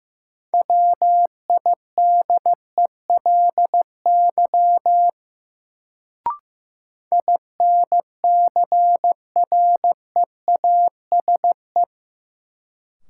Morse code, heard as a keyed tone, about 15 wpm, 700 Hz. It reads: WIDELY INCREASE
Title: Sentences from Top 1200 Words - New Word Drill Alt - 15wpm